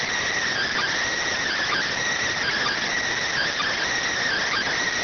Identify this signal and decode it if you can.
Emergency Alert System (SAME) tones, heard as harsh buzzing bursts. It is heard from 0.5 to 4.7 s.
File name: findings / SAGITTAL 43mm 4mm.gen-06 rx